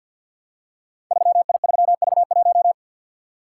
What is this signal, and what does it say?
Morse code, heard as a keyed tone, about 53 wpm, 700 Hz. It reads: 3I3V1
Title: Code Groups - 5 Letter:Numbers - 50wpm